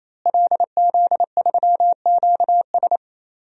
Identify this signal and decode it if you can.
Morse code, heard as a keyed tone, about 28 wpm, 700 Hz. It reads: LZ3QS
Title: DXCC-100 Call Signs - 5 Characters - Repeated 3x - 28wpm